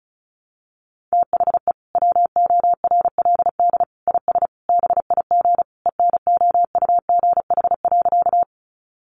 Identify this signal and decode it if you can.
Morse code, heard as a keyed tone, about 34 wpm, 700 Hz. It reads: THE WORLD IS BIG ENOUGH.
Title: Sentences from Top 400 Words 35wpm